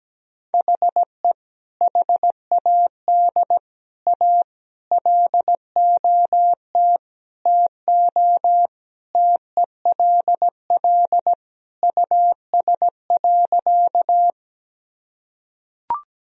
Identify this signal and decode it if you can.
Morse code, heard as a keyed tone, about 17 wpm, 700 Hz. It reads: HE HAD A LOT TO TELL US.
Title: Sentences from Top 1000 Words - Review Part 3 - 17wpm